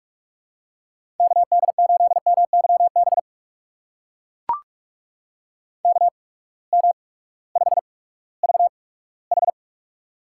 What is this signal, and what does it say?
Morse code, heard as a keyed tone, about 42 wpm, spaced as for 20 wpm, 700 Hz. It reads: KD8MYB KM5VH
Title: USA Call Signs - 8x Character Spacing - 45wpm